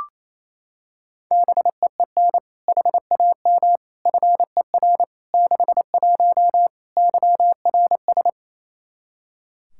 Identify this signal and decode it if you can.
Morse code, heard as a keyed tone, about 28 wpm, 700 Hz. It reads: BEEN HAM FER 61 YRS
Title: Practice QSO - Exchange 3 - More Info - 28wpm